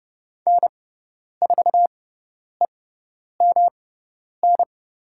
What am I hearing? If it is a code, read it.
Morse code, heard as a keyed tone, about 30 wpm, spaced as for 10 wpm, 700 Hz. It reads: N4EMN
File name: Lesson 18g - Letter m - Call Signs 30wpm